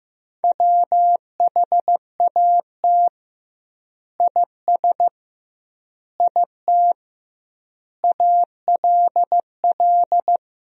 Morse code, heard as a keyed tone, about 15 wpm, 700 Hz. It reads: WHAT IS IT ALL
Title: 2x Word-Spacing - Sentences from Top 100 Words - 15wpm